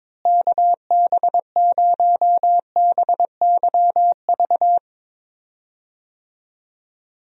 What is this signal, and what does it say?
Morse code, heard as a keyed tone, about 22 wpm, 700 Hz. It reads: KB0BYV
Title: USA Call Signs - 6 Characters - Repeated 3x - 22wpm